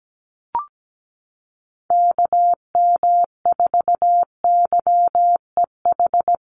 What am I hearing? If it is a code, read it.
Morse code, heard as a keyed tone, about 17 wpm, 700 Hz. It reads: KM4YEH
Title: USA Call Signs Repeated 2x 17wpm